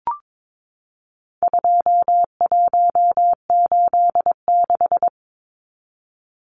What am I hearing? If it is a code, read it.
Morse code, heard as a keyed tone, about 22 wpm, 700 Hz. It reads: 2186
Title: Numbers - 4-Digit 22wpm